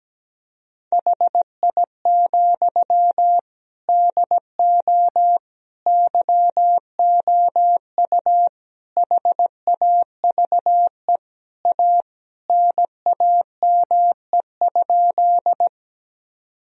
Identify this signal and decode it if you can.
Morse code, heard as a keyed tone, about 17 wpm, 700 Hz. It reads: HI, DO YOU HAVE A NAME?
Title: Sentences from Top 1000 Words - Review Part 3 - 17wpm